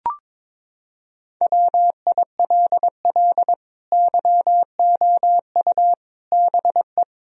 Morse code, heard as a keyed tone, about 22 wpm, 700 Hz. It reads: WILL YOU BE
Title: Sentences from Top 1000 Words - Review Part 2 - 22wpm